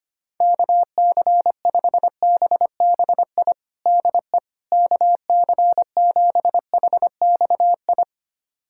Morse code, heard as a keyed tone, about 25 wpm, 700 Hz. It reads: KC5BBI DE KC7HXI
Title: USA Callsign DE Callsign - 6 Characters - 25wpm